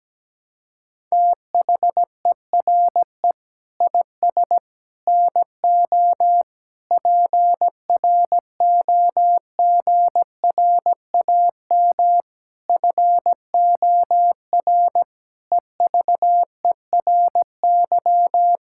Morse code, heard as a keyed tone, about 17 wpm, 700 Hz. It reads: THERE IS NO PROGRAM FOR EVERY
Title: Sentences from Top 1000 Words - Review Part 5 - 17wpm